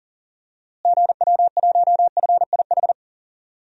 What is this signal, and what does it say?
Morse code, heard as a keyed tone, about 40 wpm, 700 Hz. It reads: GW1FIH
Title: DXCC-100 Call Signs - 6 Characters - Repeated 3x - 40wpm